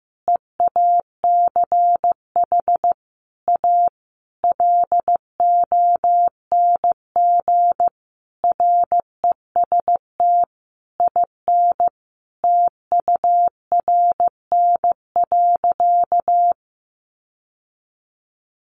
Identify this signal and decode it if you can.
Morse code, heard as a keyed tone, about 15 wpm, 700 Hz. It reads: EACH A LONG REST IN TURN.
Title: Sentences from Top 700 Words 15wpm